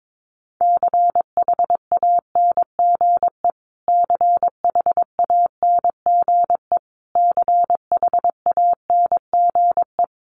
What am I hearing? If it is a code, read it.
Morse code, heard as a keyed tone, about 22 wpm, 700 Hz. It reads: CHANGE CHANGE CHANGE
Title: Top 300 Words Repeated 22wpm